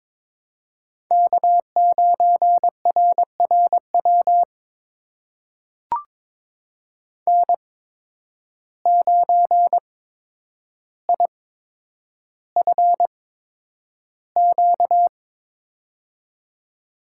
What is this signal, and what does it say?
Morse code, heard as a keyed tone, about 23 wpm, spaced as for 10 wpm, 700 Hz. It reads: K9RRW N9IFQ
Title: USA Call Signs - 5 Characters - 8x Character Spacing - 22wpm